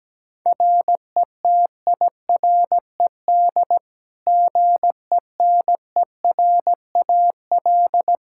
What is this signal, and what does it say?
Morse code, heard as a keyed tone, about 17 wpm, 700 Hz. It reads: RETIRED GENERAL
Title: Practice QSO - Exchange 3 - More Info - 17wpm